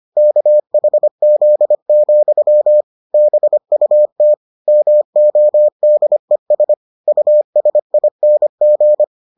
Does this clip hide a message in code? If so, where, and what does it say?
Yes: Morse code from 0.2 to 9.0 s, heard as a keyed tone, about 25 wpm, 600 Hz. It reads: KHZ, BUT MODES USING